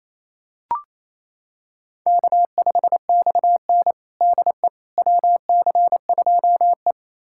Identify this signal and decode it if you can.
Morse code, heard as a keyed tone, about 28 wpm, 700 Hz. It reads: K5XN DE WC2E
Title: USA Callsign DE Callsign - 4 Characters - 28wpm